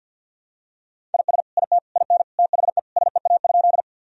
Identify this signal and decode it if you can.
Morse code, heard as a keyed tone, about 48 wpm, 700 Hz. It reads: IS IT IN THE SEA?
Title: Sentences from Top 1000 Words - Review Part 1 - 50wpm